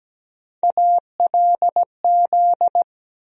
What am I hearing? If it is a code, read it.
Morse code, heard as a keyed tone, about 17 wpm, 700 Hz. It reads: ALZ
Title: USA Call Sign Postfixes - 3 Letters 17wpm